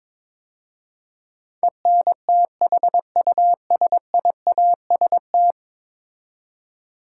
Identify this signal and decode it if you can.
Morse code, heard as a keyed tone, about 22 wpm, 700 Hz. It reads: ENTHUSIAST
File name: practice / 22wpm / 10 Letter Words - First 3 Letters Given - 22wpm